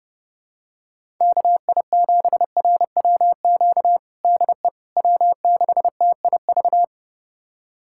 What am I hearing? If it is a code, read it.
Morse code, heard as a keyed tone, about 30 wpm, 700 Hz. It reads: KI7RWQ DE W6TIV
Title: USA Callsign DE Callsign 30wpm